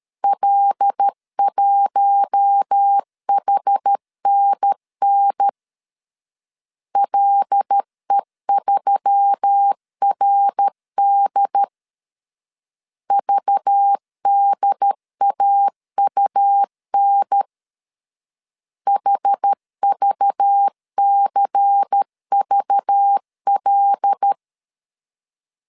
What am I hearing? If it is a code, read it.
Morse code, heard as a keyed tone, about 13 wpm, 800 Hz. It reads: L1HNN LE3RD VDAUN HVCVL
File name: CW / DET / 1 6 1500C S A 12wpm 3440